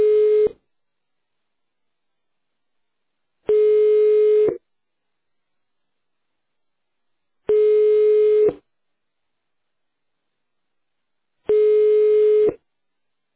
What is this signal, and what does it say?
Ringback tone (425 Hz, 1 s on, 3 s off).